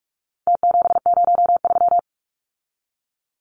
Morse code, heard as a keyed tone, about 44 wpm, 700 Hz. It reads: T703
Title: Numbers - Contest Style - 4-Digit 45wpm